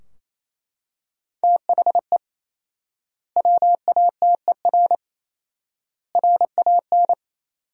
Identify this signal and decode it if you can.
Morse code, heard as a keyed tone, about 28 wpm, 700 Hz. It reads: THE WATER RAN
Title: 4x Word-Spacing - Sentences from Top 500 Words - 28wpm